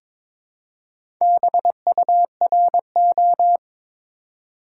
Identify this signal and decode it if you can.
Morse code, heard as a keyed tone, about 22 wpm, 700 Hz. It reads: BURO
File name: Top 90 QSO Elements - 2x Character Spacing - 22wpm